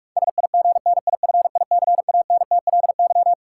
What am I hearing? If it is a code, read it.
Morse code, heard as a keyed tone, about 46 wpm, 700 Hz. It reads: SIGNIFICANTLY
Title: Sentences from Top 1500 Words - New Word Drill - 45wpm